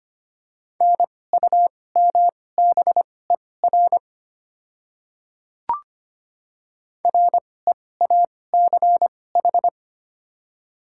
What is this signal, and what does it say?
Morse code, heard as a keyed tone, about 25 wpm, spaced as for 18 wpm, 690 Hz. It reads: NUMBER REACH
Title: 2x Character Spacing - Top 500 Words - 25wpm